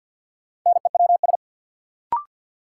Morse code, heard as a keyed tone, about 46 wpm, 700 Hz. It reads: NEWS
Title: Sentences from Top 800 Words - New Word Drill - 50wpm